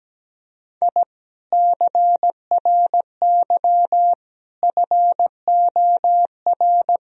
Morse code, heard as a keyed tone, about 17 wpm, 700 Hz. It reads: I CRY FOR
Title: Sentences from Top 1000 Words - Review Part 2 - 17wpm